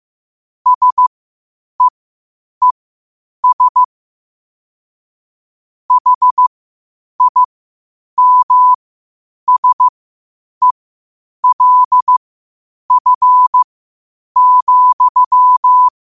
Morse code, heard as a keyed tone, about 15 wpm, spaced as for 8 wpm, 1000 Hz. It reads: SEES HIMSELF,
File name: qotdfiles / 8wpm